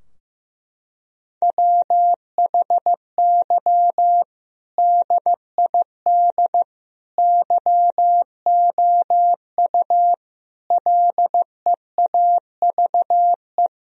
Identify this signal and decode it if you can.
Morse code, heard as a keyed tone, about 15 wpm, 700 Hz. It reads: WHY DID YOU LEAVE
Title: Sentences from Top 1000 Words - Review Part 3 - 15wpm